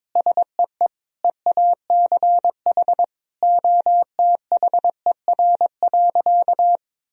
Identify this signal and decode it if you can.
Morse code, heard as a keyed tone, about 22 wpm, 700 Hz. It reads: SEE EACH OTHER.
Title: Sentences from Top 1000 Words - Review Part 1 - 22wpm